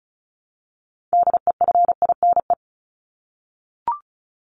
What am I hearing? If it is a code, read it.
Morse code, heard as a keyed tone, about 34 wpm, 700 Hz. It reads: DEFINE
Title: ED Suffix - Top 100 Words with Base Words 35wpm